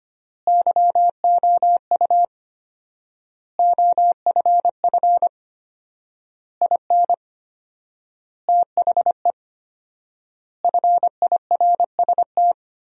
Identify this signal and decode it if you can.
Morse code, heard as a keyed tone, about 25 wpm, 700 Hz. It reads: YOU OFF IN THE FIRST
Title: 4x Word-Spacing - Sentences from Top 200 Words - 25wpm